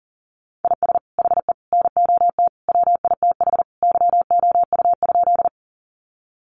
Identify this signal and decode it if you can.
Morse code, heard as a keyed tone, about 40 wpm, 700 Hz. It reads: IS HE NOT WITH YOU?
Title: Sentences from Top 600 Words 40wpm